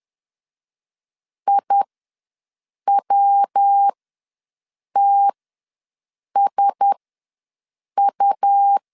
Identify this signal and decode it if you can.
Morse code, heard as a keyed tone, about 10 wpm, spaced as for 6 wpm, 800 Hz. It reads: IWTSU
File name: LEZIONI / 5 LBVC / 5 L ws 10wpm 3636